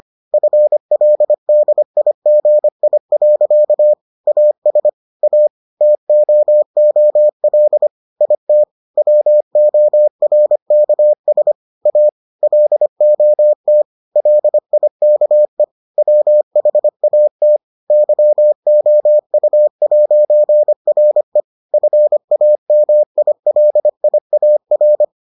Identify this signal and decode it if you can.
Morse code, heard as a keyed tone, about 25 wpm, 600 Hz. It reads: FLDIGI. AS A TOOL IT WORKS A LOT LIKE WHAT YOU'RE FAMILIAR